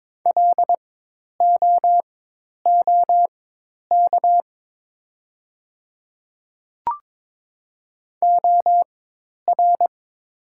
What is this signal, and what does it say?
Morse code, heard as a keyed tone, about 22 wpm, spaced as for 10 wpm, 700 Hz. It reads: LOOK OR
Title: 4x Character Spacing - Top 100 Words - 22wpm